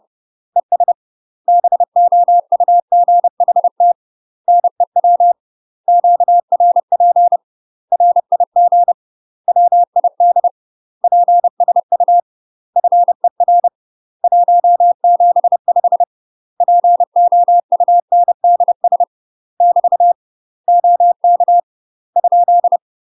Morse code, heard as a keyed tone, about 30 wpm, 700 Hz. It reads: ES BOUGHT NEW QRP RIG WID PSU FER 175 POUNDS <BT> OK ?
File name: UK MASTER QSOs TRAINING / 30WPM / UK MASTER QSOs 0026